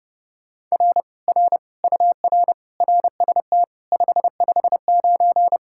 Morse code, heard as a keyed tone, about 37 wpm, 700 Hz. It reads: R R UR RST 559
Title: Speed-Racing Common QSO Phrases Encore - 30wpm